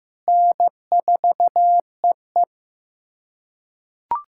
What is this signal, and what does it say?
Morse code, heard as a keyed tone, about 15 wpm, 700 Hz. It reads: N4EE
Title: USA Callsign DE Callsign - 4 Characters - 15wpm